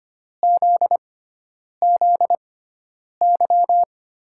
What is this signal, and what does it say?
Morse code, heard as a keyed tone, about 25 wpm, spaced as for 9 wpm, 700 Hz. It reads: ZZY
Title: USA Call Signs - 6 Characters - 6x Character Spacing - 25wpm